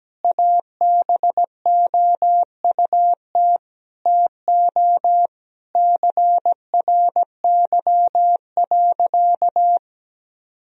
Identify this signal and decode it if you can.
Morse code, heard as a keyed tone, about 17 wpm, 700 Hz. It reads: ABOUT TO CRY.
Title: Sentences from Top 1000 Words - Review Part 2 - 17wpm